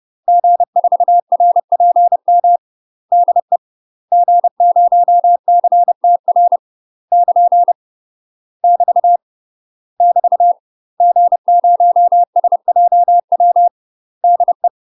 Morse code, heard as a keyed tone, about 30 wpm, 700 Hz. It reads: G4RPM DE G0CTR <KN> <BT> <BT> G0SJW DE
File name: UK MASTER QSOs TRAINING / 30WPM / UK MASTER QSOs 0021